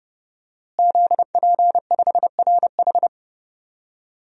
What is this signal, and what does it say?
Morse code, heard as a keyed tone, about 30 wpm, 700 Hz. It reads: ZP5RH
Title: DXCC-100 Call Signs - 5 Characters - Repeated 3x - 30wpm